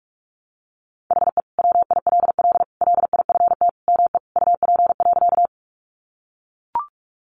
Morse code, heard as a keyed tone, about 44 wpm, 700 Hz. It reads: HE WILL LIFT ME UP.
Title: Sentences from Top 1000 Words 45wpm